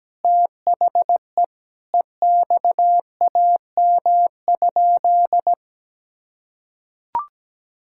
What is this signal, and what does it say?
Morse code, heard as a keyed tone, about 17 wpm, 700 Hz. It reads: THE EXAM?
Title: Sentences from Top 1000 Words - Review Part 5 - 17wpm